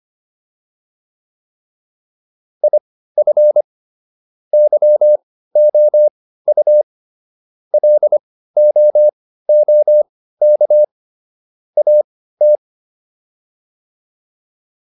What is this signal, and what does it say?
Morse code, heard as a keyed tone, about 25 wpm, spaced as for 15 wpm, 600 Hz. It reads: IF YOU LOOK AT